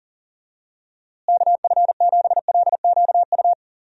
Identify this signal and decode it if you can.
Morse code, heard as a keyed tone, about 40 wpm, 700 Hz. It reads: KF7LQU
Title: USA Call Signs - 6 Characters - Repeated 2x - 40wpm